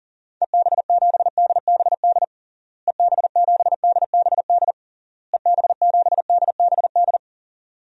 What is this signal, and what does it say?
Morse code, heard as a keyed tone, about 40 wpm, 700 Hz. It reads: EB7DBD EB7DBD EB7DBD
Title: DXCC-100 Call Signs - 6 Characters - Repeated 3x - 40wpm